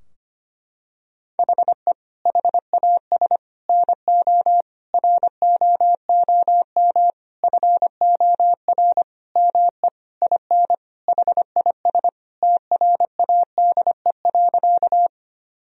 Morse code, heard as a keyed tone, about 25 wpm, 700 Hz. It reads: HE HAS NO ROOM FOR ME IN HIS TRADE.
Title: Sentences from Top 1000 Words - Review Part 4 - 25wpm